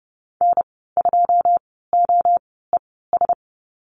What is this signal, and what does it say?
Morse code, heard as a keyed tone, about 30 wpm, spaced as for 17 wpm, 700 Hz. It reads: N2OES
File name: Lesson 14g - Number 2 - Call Signs 30wpm